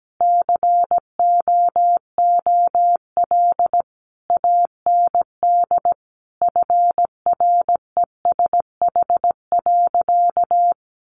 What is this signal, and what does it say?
Morse code, heard as a keyed tone, about 17 wpm, 700 Hz. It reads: COOL AND FRESH.
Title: Sentences from Top 1200 Words 17wpm